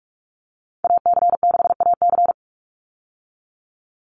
Morse code, heard as a keyed tone, about 44 wpm, 700 Hz. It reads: AC6AC
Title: Call Signs - 45wpm